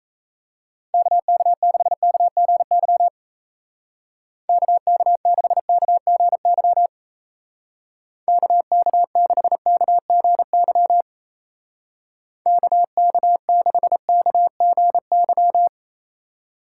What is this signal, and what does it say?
Morse code, heard as a keyed tone, about 34 wpm, 700 Hz. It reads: KK6KGY KK6KGY KK6KGY KK6KGY
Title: Speed Racing - USA 2x3 Call Signs 28wpm